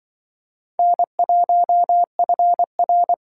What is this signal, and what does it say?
Morse code, heard as a keyed tone, about 24 wpm, 700 Hz. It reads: N1FR
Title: Speed Racing - USA 1x2 Call Signs 20wpm